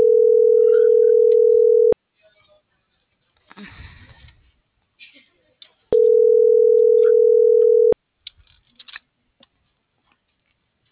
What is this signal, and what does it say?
Ringback tone (North America), 440 + 480 Hz.